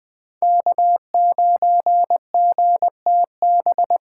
Morse code, heard as a keyed tone, about 20 wpm, 700 Hz. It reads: K9GTB
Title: Speed Racing - USA 1x3 Call Signs 17wpm